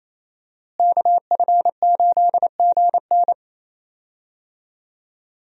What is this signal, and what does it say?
Morse code, heard as a keyed tone, about 28 wpm, 700 Hz. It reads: KF8GN
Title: USA Call Signs - 5 Characters - Repeated 2x - 28wpm